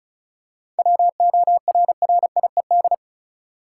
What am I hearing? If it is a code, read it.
Morse code, heard as a keyed tone, about 36 wpm, 700 Hz. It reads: WORRIED
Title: Seven Letter Words - 35wpm